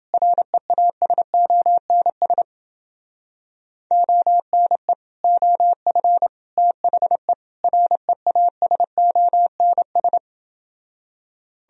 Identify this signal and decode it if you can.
Morse code, heard as a keyed tone, about 29 wpm, 700 Hz. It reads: REASONS ONE OF THE REASONS